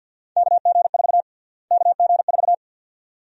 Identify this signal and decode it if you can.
Morse code, heard as a keyed tone, about 49 wpm, 700 Hz. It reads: KG4 KG4
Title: USA Call Sign Prefixes - Repeated 2x - 50wpm